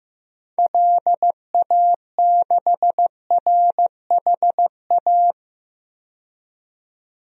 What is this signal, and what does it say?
Morse code, heard as a keyed tone, about 15 wpm, 700 Hz. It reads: LA6RHA
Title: DXCC-100 Call Signs - 6 Characters - 15wpm